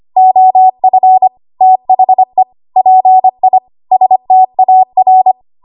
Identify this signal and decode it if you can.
Morse code, heard as a keyed tone, about 25 wpm, 750 Hz. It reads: OF THE PI STAR
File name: Archive / 25 WPM / 25WPM